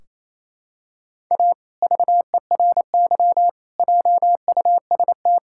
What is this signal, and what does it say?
Morse code, heard as a keyed tone, about 28 wpm, 700 Hz. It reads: A VERY JUST
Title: Sentences from Top 1600 Words 28wpm